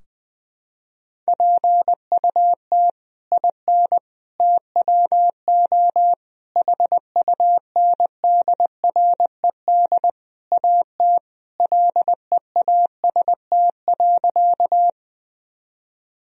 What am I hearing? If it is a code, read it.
Morse code, heard as a keyed tone, about 20 wpm, 700 Hz. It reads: PUT IN TWO HUNDRED AT LEAST.
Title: Sentences from Top 1000 Words - Review Part 2 - 20wpm